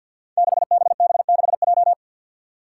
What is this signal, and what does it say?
Morse code, heard as a keyed tone, about 50 wpm, 700 Hz. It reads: 6BB6J
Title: Five Challenging Letter-Number Combinations 50wpm